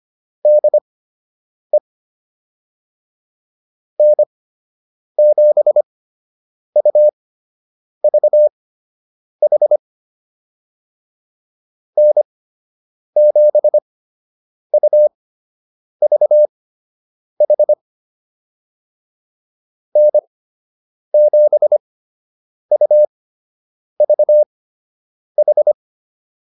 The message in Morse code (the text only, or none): DE N7UVH N7UVH N7UVH